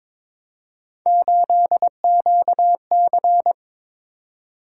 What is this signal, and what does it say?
Morse code, heard as a keyed tone, about 23 wpm, 700 Hz. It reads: 8QC
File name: practice / 22wpm / Three Letter-Number Combinations - 22wpm